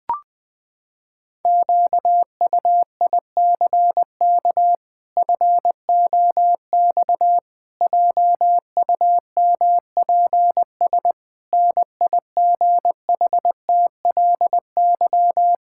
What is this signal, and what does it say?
Morse code, heard as a keyed tone, about 21 wpm, 700 Hz. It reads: QUICK FOX JUMPS NIGHTLY
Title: Pangram Challenge v2 - 20wpm